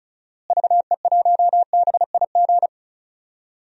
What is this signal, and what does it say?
Morse code, heard as a keyed tone, about 36 wpm, 700 Hz. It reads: VE1BIG
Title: DXCC-100 Call Signs - 6 Characters - Repeated 3x - 35wpm